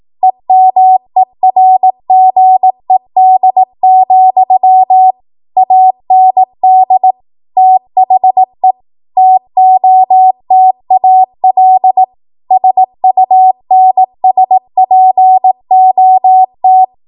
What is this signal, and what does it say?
Morse code, heard as a keyed tone, about 18 wpm, 750 Hz. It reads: EMERGED, AND THE TOTAL SUNSPOT
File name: Archive / 18 WPM / 18WPM